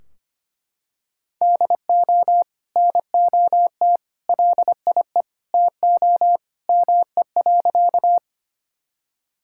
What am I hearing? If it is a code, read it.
Morse code, heard as a keyed tone, about 25 wpm, 700 Hz. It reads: DO NOT LIE TO ME.
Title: Sentences from Top 1600 Words 25wpm